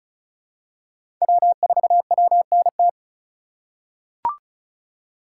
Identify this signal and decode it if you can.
Morse code, heard as a keyed tone, about 34 wpm, 700 Hz. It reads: W4WNT
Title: K1USN SST QSO Practice - 35wpm